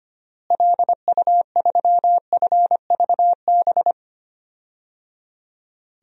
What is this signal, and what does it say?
Morse code, heard as a keyed tone, about 25 wpm, 700 Hz. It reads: LU3FVB
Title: DXCC-100 Call Signs - 6 Characters - Repeated 3x - 25wpm